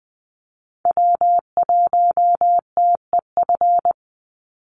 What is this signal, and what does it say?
Morse code, heard as a keyed tone, about 20 wpm, 700 Hz. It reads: W1TEF
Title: Call Signs Repeated 2x - 20wpm